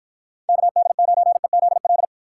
Morse code, heard as a keyed tone, about 54 wpm, 700 Hz. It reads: KD9EZL